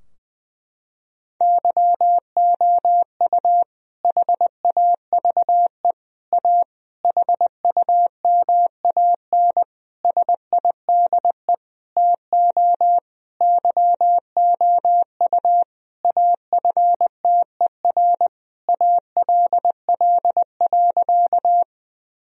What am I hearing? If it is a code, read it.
Morse code, heard as a keyed tone, about 20 wpm, 700 Hz. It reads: YOU HAVE A HUMAN SIDE TO YOU AFTER ALL.
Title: Sentences from Top 1000 Words - Review Part 3 - 20wpm